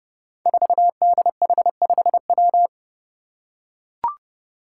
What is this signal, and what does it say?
Morse code, heard as a keyed tone, about 30 wpm, 700 Hz. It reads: 4DH5W
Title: Five Challenging Letter-Number Combinations 30wpm